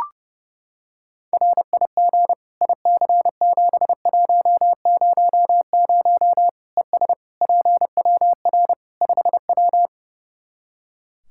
Morse code, heard as a keyed tone, about 30 wpm, 700 Hz. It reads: RIG IC7100 ES PWR 5W
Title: Practice QSO - Exchange 2 - Info - 30wpm